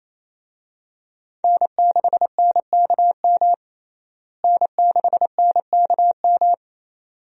Morse code, heard as a keyed tone, about 28 wpm, 700 Hz. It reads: N6NKM N6NKM
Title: USA Call Signs - 5 Characters - Repeated 2x - 28wpm